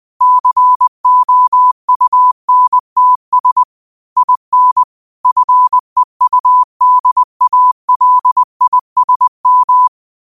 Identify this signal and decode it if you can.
Morse code, heard as a keyed tone, about 20 wpm, 1000 Hz. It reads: COUNTS IN FEUDALISM
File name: 2025-06-25-20wpm